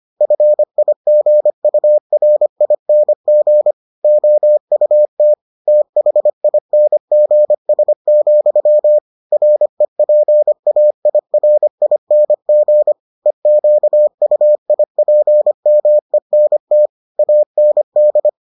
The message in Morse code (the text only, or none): FIGURING OUT THINGS, REPAIRING EQUIPMENT AND